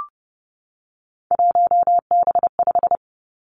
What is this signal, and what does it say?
Morse code, heard as a keyed tone, about 30 wpm, 700 Hz. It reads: 1B5
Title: Three Challenging Letter-Number Combinations 30wpm